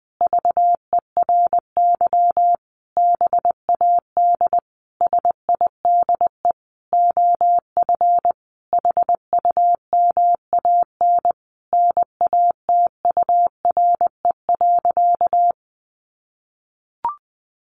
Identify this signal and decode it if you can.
Morse code, heard as a keyed tone, about 20 wpm, 700 Hz. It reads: VERY BAD SIDE OF HUMAN NATURE.
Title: Sentences from Top 700 Words 20wpm